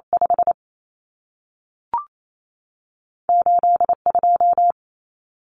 Morse code, heard as a keyed tone, about 28 wpm, spaced as for 3 wpm, 700 Hz. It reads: 58 2
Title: Common Cut Numbers - 2-Digit 28wpm